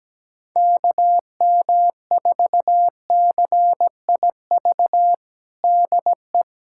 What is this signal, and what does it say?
Morse code, heard as a keyed tone, about 17 wpm, 700 Hz. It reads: KM4CIV DE
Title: USA Callsign DE Callsign 17wpm